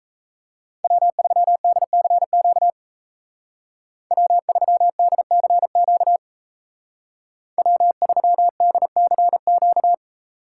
Morse code, heard as a keyed tone, about 34 wpm, 700 Hz. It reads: W3DCQ W3DCQ W3DCQ
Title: Speed Racing - USA 1x3 Call Signs 28wpm